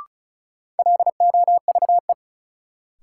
Morse code, heard as a keyed tone, about 36 wpm, 700 Hz. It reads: LOVE
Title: Sets of 5 Words Encore - New Word Drill - 35wpm